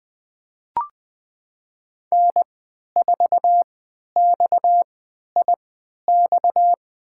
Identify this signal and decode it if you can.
Morse code, heard as a keyed tone, about 20 wpm, spaced as for 11 wpm, 700 Hz. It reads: N4XIX
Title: USA Call Signs - 5 Characters - 3x Character Spacing - 20wpm